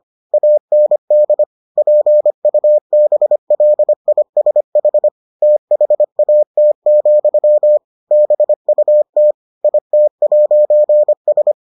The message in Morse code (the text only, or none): AND PUBLISH THAT, BUT IT'S